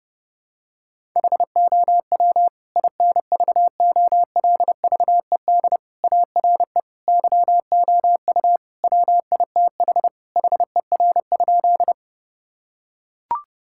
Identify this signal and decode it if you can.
Morse code, heard as a keyed tone, about 29 wpm, 700 Hz. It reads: HOW INVOLVED ARE YOU WITH HER?
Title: Sentences from Top 1000 Words - Review Part 5 - 30wpm